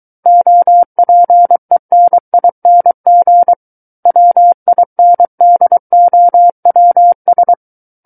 Morse code, heard as a keyed tone, about 23 wpm, 700 Hz. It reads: OPENING WINDOWS
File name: Book One / 14-Chapter 14 - 23 WPM